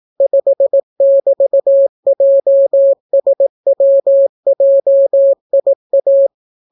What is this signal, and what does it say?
Morse code, heard as a keyed tone, about 18 wpm, 550 Hz. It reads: 5<BT>JSWJIA